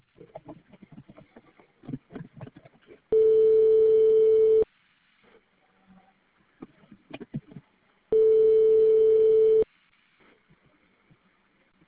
Ringback tone (France), 440 Hz.